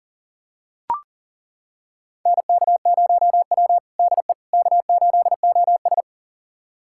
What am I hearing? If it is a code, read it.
Morse code, heard as a keyed tone, about 37 wpm, 700 Hz. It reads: NK0W DE K8OS
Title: USA Callsign DE Callsign - 4 Characters - 40wpm